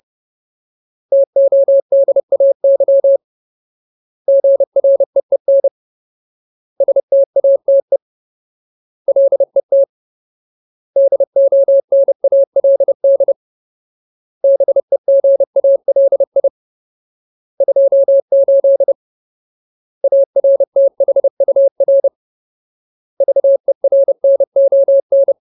Morse code, heard as a keyed tone, about 30 wpm, 550 Hz. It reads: TODAY GREEN STATE LET DONALD BEGALI 28 ARTHUR VERNON
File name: Ultimate Word List / 30WPM Ultimate